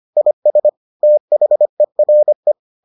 Morse code, heard as a keyed tone, about 25 wpm, 600 Hz. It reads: IS THERE